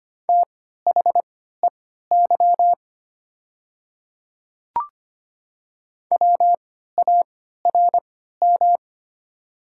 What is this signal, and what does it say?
Morse code, heard as a keyed tone, about 25 wpm, spaced as for 14 wpm, 690 Hz. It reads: THEY WARM
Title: 3x Character Spacing - Top 500 Words - 25wpm